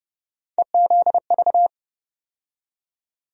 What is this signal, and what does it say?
Morse code, heard as a keyed tone, about 30 wpm, 700 Hz. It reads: EZV
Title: USA Call Sign Postfixes - 3 Letters 30wpm